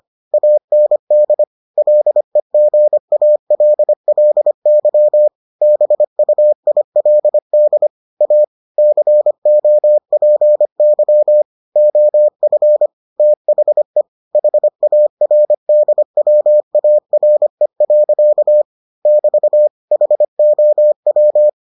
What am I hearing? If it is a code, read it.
Morse code, heard as a keyed tone, about 25 wpm, 600 Hz. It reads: AND LEGALLY BUILD A COPY OF THE HARDWARE. <BT> HOW